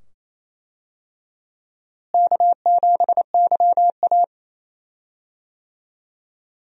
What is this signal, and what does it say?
Morse code, heard as a keyed tone, about 28 wpm, 700 Hz. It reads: K7YA
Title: USA Call Signs - 4 Characters - Repeated 2x - 28wpm